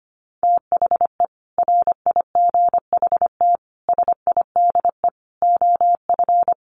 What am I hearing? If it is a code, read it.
Morse code, heard as a keyed tone, about 25 wpm, 700 Hz. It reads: THE RIGHT SIDE OF
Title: Sets of 5 Words Encore 25wpm